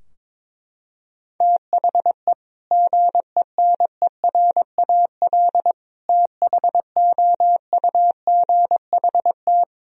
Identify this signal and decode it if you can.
Morse code, heard as a keyed tone, about 22 wpm, 700 Hz. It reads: THE GENERAL THOUGHT
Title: Sentences from Top 1000 Words - Review Part 3 - 22wpm